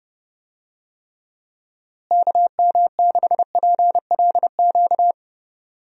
Morse code, heard as a keyed tone, about 30 wpm, 700 Hz. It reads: KM6PLQ
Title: USA Call Signs - 6 Characters - Repeated 3x - 30wpm